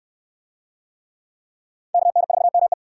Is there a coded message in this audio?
Yes: Morse code from 1.9 to 2.7 s, heard as a keyed tone, about 61 wpm, 700 Hz. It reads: KN4ME